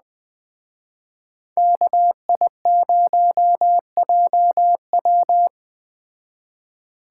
Morse code, heard as a keyed tone, about 21 wpm, 700 Hz. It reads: KI0JW